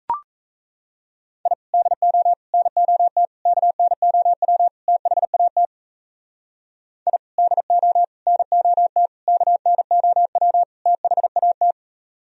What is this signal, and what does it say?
Morse code, heard as a keyed tone, about 34 wpm, 700 Hz. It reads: I DO NOT KNOW THAT I DO NOT KNOW THAT
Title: Speed-Racing - Sets of 5 Words 28wpm